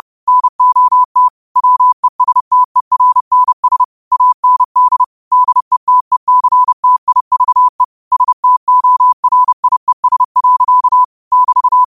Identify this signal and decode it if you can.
Morse code, heard as a keyed tone, about 30 wpm, 1000 Hz. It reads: NOT WESTERNS AND DETECTIVE STORIES. <BT>
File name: qotdfiles / 30wpm